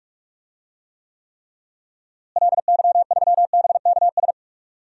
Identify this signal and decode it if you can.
Morse code, heard as a keyed tone, about 46 wpm, 700 Hz. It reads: LY3BKS
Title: DXCC-100 Call Signs - 6 Characters - Repeated 3x - 45wpm